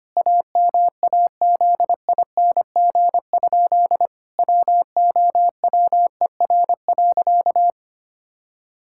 Morse code, heard as a keyed tone, about 25 wpm, 700 Hz. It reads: AMAZING? WOWER.
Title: Pangram Challenge v2 - 25wpm